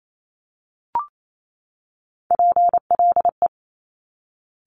Morse code, heard as a keyed tone, about 28 wpm, 700 Hz. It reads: PLE
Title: Top Three Letter Combinations in Words 28wpm